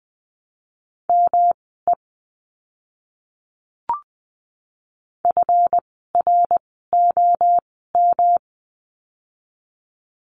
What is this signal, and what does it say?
Morse code, heard as a keyed tone, about 20 wpm, spaced as for 14 wpm, 700 Hz. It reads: ME FROM
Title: 237 Lesson 38f - Letter q - Words 20wpm